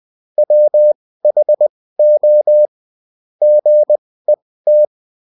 Morse code, heard as a keyed tone, about 20 wpm, spaced as for 15 wpm, 600 Hz. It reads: WHO GET
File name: Top-100-Words-15